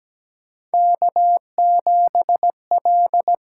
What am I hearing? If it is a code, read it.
Morse code, heard as a keyed tone, about 17 wpm, 700 Hz. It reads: K7L
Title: USA Call Signs - 5 Characters - 6x Character Spacing - 17wpm